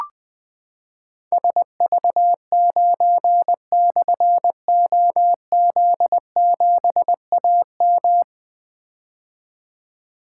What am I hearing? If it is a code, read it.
Morse code, heard as a keyed tone, about 20 wpm, 700 Hz. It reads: SV9/OZ7AM
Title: Difficult Call Signs 20wpm